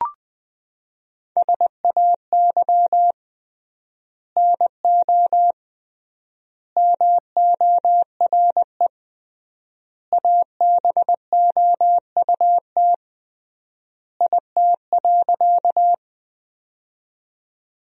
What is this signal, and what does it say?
Morse code, heard as a keyed tone, about 21 wpm, 700 Hz. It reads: SAY NO MORE ABOUT IT.
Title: 3x Word-Spacing - Sentences from Top 200 Words - 20wpm